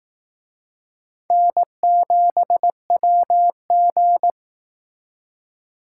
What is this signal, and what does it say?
Morse code, heard as a keyed tone, about 18 wpm, 700 Hz. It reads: N7WG
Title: Speed Racing - USA 1x2 Call Signs 15wpm